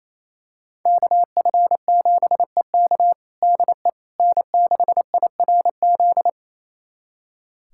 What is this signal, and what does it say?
Morse code, heard as a keyed tone, about 28 wpm, 700 Hz. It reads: KF7EK DE N6IRZ
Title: USA Callsign DE Callsign - 5 Characters - 28wpm